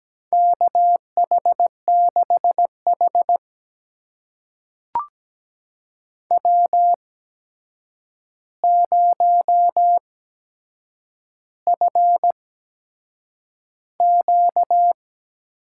Morse code, heard as a keyed tone, about 17 wpm, spaced as for 7 wpm, 700 Hz. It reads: KH6H W0FQ
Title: USA Call Signs - 4 Characters - 8x Character Spacing - 17wpm